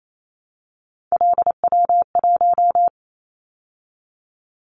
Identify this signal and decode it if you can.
Morse code, heard as a keyed tone, about 28 wpm, 700 Hz. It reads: LW1
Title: Callsign Prefixes for Top 50 DXCC - 28wpm